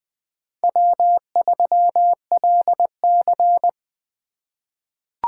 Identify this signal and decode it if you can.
Morse code, heard as a keyed tone, about 20 wpm, 700 Hz. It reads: W3LC
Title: USA Call Signs - 4 Characters - 3x Character Spacing - 20wpm